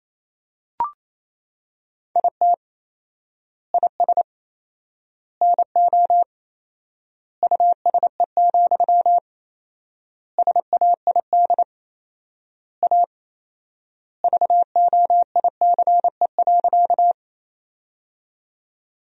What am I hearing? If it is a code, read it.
Morse code, heard as a keyed tone, about 28 wpm, 700 Hz. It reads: IT IS NO USE, SAID A VOICE.
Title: 4x Word-Spacing - Sentences from Top 500 Words - 28wpm